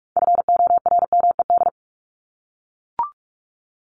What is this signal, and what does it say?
Morse code, heard as a keyed tone, about 44 wpm, 700 Hz. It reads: FORMED
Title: ED Suffix - Top 100 Words with Base Words 45wpm